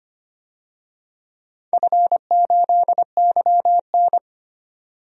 Morse code, heard as a keyed tone, about 25 wpm, 700 Hz. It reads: F8YN
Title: DXCC-100 Call Signs - 4 Characters - 25wpm